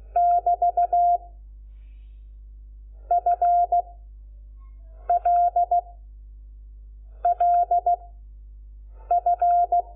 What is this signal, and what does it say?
Morse code, heard as a keyed tone, about 16 wpm, 700 Hz. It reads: <BT> F L L F
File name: Lecciones / CLASE Nº5